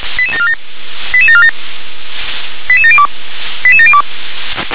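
Five-tone selective call (ZVEI), heard as quick runs of five tones, tones of about 70 ms. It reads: 88747 88702 88702
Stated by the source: Five-tone ZVEI selective call 88083 received on an SDR.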